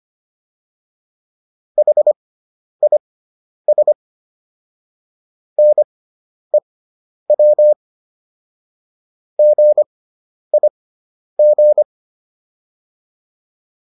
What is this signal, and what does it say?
Morse code, heard as a keyed tone, about 25 wpm, spaced as for 10 wpm, 600 Hz. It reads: HIS NEW GIG